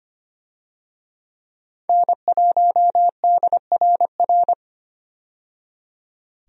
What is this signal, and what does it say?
Morse code, heard as a keyed tone, about 25 wpm, 700 Hz. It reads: N1DRR
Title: USA Call Signs - 5 Characters - 25wpm